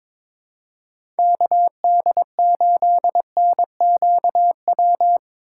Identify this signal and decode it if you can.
Morse code, heard as a keyed tone, about 22 wpm, 700 Hz. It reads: KD8NQW
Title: USA Call Signs - 6 Characters - 8x Character Spacing - 22wpm